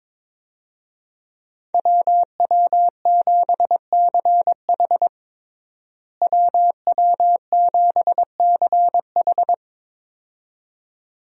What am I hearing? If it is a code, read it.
Morse code, heard as a keyed tone, about 22 wpm, 700 Hz. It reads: WW7CH WW7CH
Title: USA Call Signs - 5 Characters - Repeated 2x - 22wpm